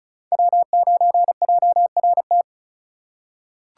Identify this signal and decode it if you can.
Morse code, heard as a keyed tone, about 34 wpm, 700 Hz. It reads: W9JRT
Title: USA Call Signs - 5 Characters - 4x Character Spacing - 35wpm